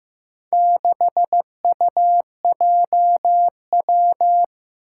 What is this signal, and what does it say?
Morse code, heard as a keyed tone, about 15 wpm, 700 Hz. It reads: 6UJW
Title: Four Challenging Letter-Number Combinations 15wpm